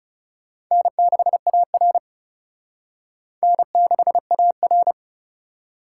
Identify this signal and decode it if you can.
Morse code, heard as a keyed tone, about 38 wpm, 700 Hz. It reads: N6AR N6AR